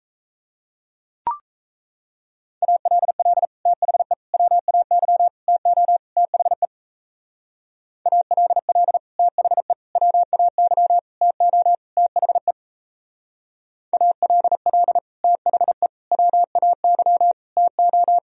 Morse code, heard as a keyed tone, about 32 wpm, 700 Hz. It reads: ALL THE WAY TO THE ALL THE WAY TO THE ALL THE WAY TO
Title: Speed-Racing - Sets of 5 Words 28wpm